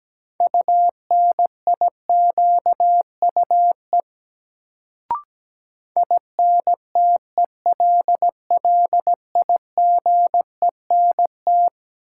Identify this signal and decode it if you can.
Morse code, heard as a keyed tone, about 17 wpm, 700 Hz. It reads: UNIQUE INTELLIGENT
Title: Sentences from Top 1000 Words - Review Part 7 - New Word Drill - 17wpm